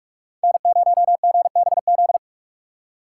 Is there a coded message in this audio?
Yes: Morse code from 0.4 to 2.2 s, heard as a keyed tone, about 41 wpm, 700 Hz. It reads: N0GBZ